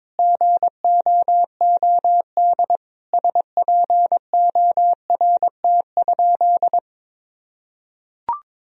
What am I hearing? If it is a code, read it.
Morse code, heard as a keyed tone, about 22 wpm, 700 Hz. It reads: GOOD SPORT?
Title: Sentences from Top 1000 Words - Review Part 4 - 22wpm